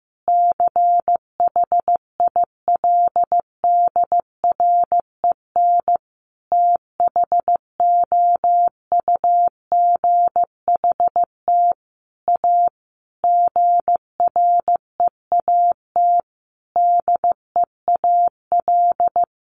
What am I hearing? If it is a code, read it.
Morse code, heard as a keyed tone, about 15 wpm, 700 Hz. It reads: CHILDREN THOUGHT A GREAT DEAL
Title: Sentences from Top 800 Words 15wpm